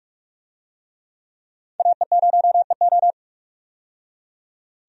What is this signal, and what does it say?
Morse code, heard as a keyed tone, about 44 wpm, 700 Hz. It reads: AE0EO
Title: USA Call Signs - 5 Characters - 45wpm